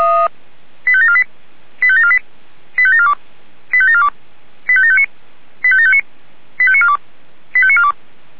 Five-tone selective call (ZVEI), heard as quick runs of five tones, tones of about 70 ms. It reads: 85748 85748 85742 85742 85759 85759 85942 85942